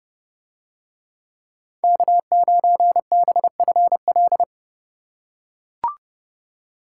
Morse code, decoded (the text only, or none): K9BFL